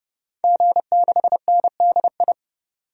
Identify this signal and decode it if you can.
Morse code, heard as a keyed tone, about 29 wpm, 700 Hz. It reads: G6NDI
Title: DXCC-100 Call Signs - 5 Characters - 30wpm